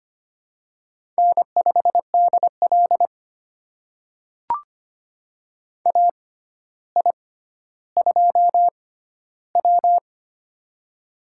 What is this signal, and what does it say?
Morse code, heard as a keyed tone, about 25 wpm, spaced as for 13 wpm, 700 Hz. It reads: N5DL AI2W
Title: USA Call Signs - 4 Characters - 6x Character Spacing - 25wpm